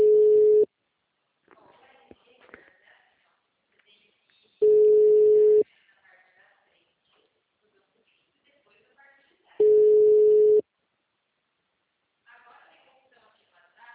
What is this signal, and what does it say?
Ringback tone (Europe, 425 Hz), with speech.
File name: out- -2013- - - .59595